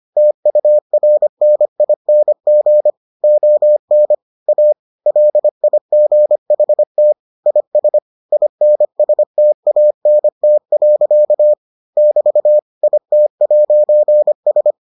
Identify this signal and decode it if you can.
Morse code, heard as a keyed tone, about 25 wpm, 600 Hz. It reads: TURNING ON A LIGHT IS INSTANT. <BT> IT'S